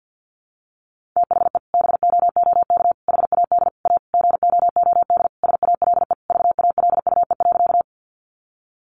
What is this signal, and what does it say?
Morse code, heard as a keyed tone, about 50 wpm, 700 Hz. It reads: THE BOOK HAD A GOOD SALE VALUE.
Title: Sentences from Top 1400 Words 50wpm